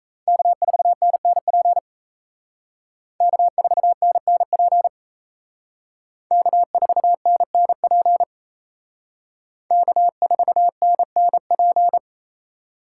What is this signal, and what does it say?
Morse code, heard as a keyed tone, about 35 wpm, 700 Hz. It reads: K4NNP K4NNP K4NNP K4NNP